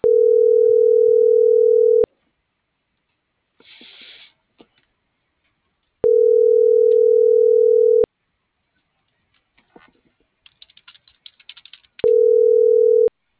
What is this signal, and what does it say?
Ringback tone (North America), 440 + 480 Hz.